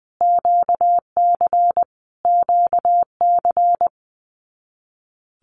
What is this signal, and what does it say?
Morse code, heard as a keyed tone, about 20 wpm, 700 Hz. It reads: QC QC
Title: Canadian Provinces and Territories - Repeated 3x - 20wpm